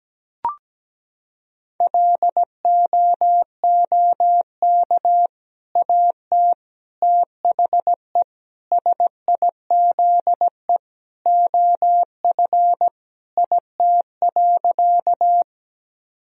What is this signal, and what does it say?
Morse code, heard as a keyed tone, about 17 wpm, 700 Hz. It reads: LOOK AT THE SIZE OF IT.
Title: Sentences from Top 1000 Words - Review Part 2 - 17wpm